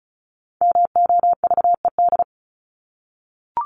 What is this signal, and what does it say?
Morse code, heard as a keyed tone, about 34 wpm, 700 Hz. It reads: MOVED
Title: ED Suffix - Top 100 Words with Base Words 35wpm